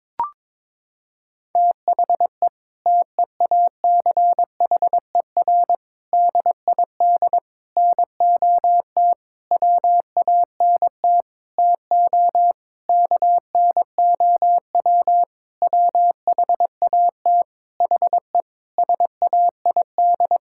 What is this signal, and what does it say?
Morse code, heard as a keyed tone, about 22 wpm, 700 Hz. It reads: THE TEACHER DID NOT WANT TO KNOW WHAT HE SAID
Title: Sentences from Top 1000 Words - Review Part 6 - 22wpm